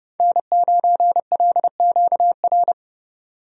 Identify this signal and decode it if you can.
Morse code, heard as a keyed tone, about 32 wpm, 700 Hz. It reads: N9LQR
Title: USA Call Signs - 5 Characters - 30wpm